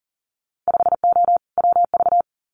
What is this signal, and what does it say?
Morse code, heard as a keyed tone, about 42 wpm, 700 Hz. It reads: 5O WV
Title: Winter Field Day Exchanges Encore 40wpm